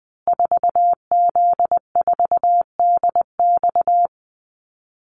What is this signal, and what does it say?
Morse code, heard as a keyed tone, about 20 wpm, 700 Hz. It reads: 4Z4DX
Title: Call Signs Repeated 3x - 20wpm